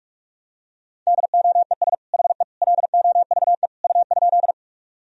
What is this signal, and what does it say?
Morse code, heard as a keyed tone, about 46 wpm, 700 Hz. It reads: DOES HE LOVE U?
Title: Sentences from Top 400 Words and Abbreviations 45wpm